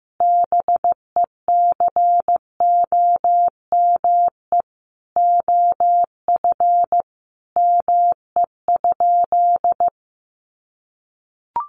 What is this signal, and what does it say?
Morse code, heard as a keyed tone, about 15 wpm, 700 Hz. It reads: BECOME OF ME?
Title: Sentences from Top 600 Words 15wpm